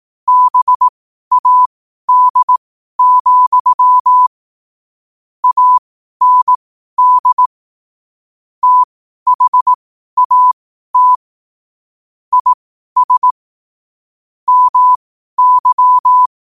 Morse code, heard as a keyed tone, about 18 wpm, spaced as for 12 wpm, 1000 Hz. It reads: BAD, AND THAT IS MY
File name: qotdfiles / 12wpm